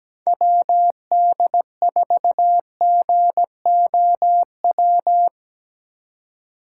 Morse code, heard as a keyed tone, about 17 wpm, 700 Hz. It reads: WD4GOW
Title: USA Call Signs - 6 Characters - 3x Character Spacing - 17wpm